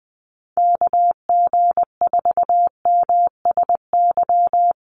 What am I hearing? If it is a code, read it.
Morse code, heard as a keyed tone, about 20 wpm, 700 Hz. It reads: KG4MSY
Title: USA Call Signs 20wpm